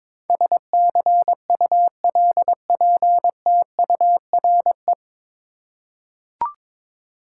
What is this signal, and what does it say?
Morse code, heard as a keyed tone, about 22 wpm, 700 Hz. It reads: SCULPTURE
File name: practice / 22wpm / Nine Letter Words - 22wpm